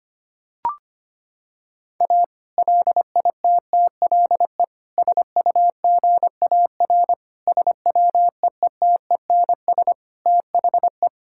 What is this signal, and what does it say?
Morse code, heard as a keyed tone, about 25 wpm, 700 Hz. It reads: A LITTLE SUGAR SWEETENS THE
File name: practice / 25wpm / Sentences with 2-Word Phrases - 25wpm